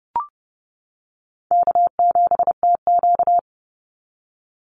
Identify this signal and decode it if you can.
Morse code, heard as a keyed tone, about 30 wpm, 700 Hz. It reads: K7TQ
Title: NA SOTA Chaser Callsigns 30wpm